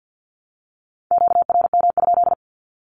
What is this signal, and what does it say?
Morse code, heard as a keyed tone, about 54 wpm, 700 Hz. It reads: QRM?
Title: Q-Codes 50wpm